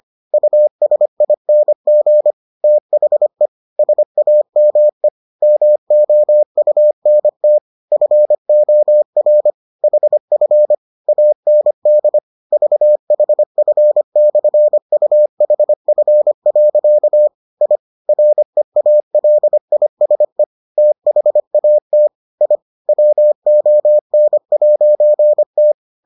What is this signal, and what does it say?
Morse code, heard as a keyed tone, about 25 wpm, 600 Hz. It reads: USING THE SAME MOUNT FOR HF AND VHF/UHF. I REALISE THAT I WON'T